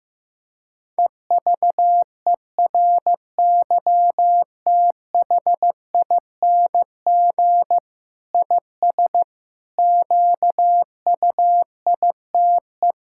Morse code, heard as a keyed tone, about 16 wpm, 700 Hz. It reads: EVERYTHING IS QUITE
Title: Sentences from Top 1000 Words - Review Part 6 - 15wpm